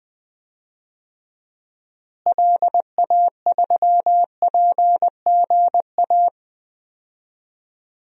Morse code, heard as a keyed tone, about 20 wpm, 700 Hz. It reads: LA3PGA
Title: DXCC-100 Call Signs - 6 Characters - Repeated 3x - 20wpm